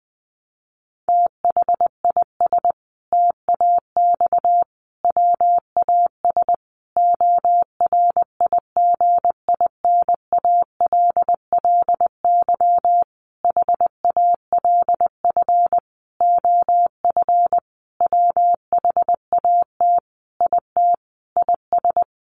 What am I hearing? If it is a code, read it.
Morse code, heard as a keyed tone, about 20 wpm, 700 Hz. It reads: THIS TAX WAS ORIGINALLY HALF OF WHAT IT IS
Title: Sentences from Top 1500 Words 20wpm